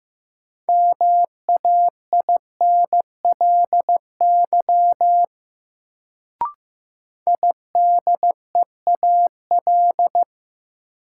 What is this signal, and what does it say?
Morse code, heard as a keyed tone, about 15 wpm, 700 Hz. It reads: MAINLY IDEAL
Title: Sentences from Top 1400 Words - New Word Drill - 15wpm